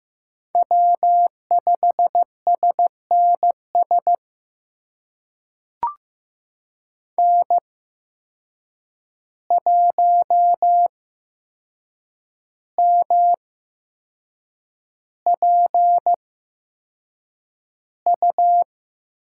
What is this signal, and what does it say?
Morse code, heard as a keyed tone, about 15 wpm, spaced as for 6 wpm, 700 Hz. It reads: W5SNS N1MPU
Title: USA Call Signs - 8x Character Spacing - 15wpm